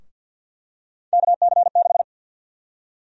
Morse code, heard as a keyed tone, about 53 wpm, 700 Hz. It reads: KC6
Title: USA Call Sign Prefixes - 50wpm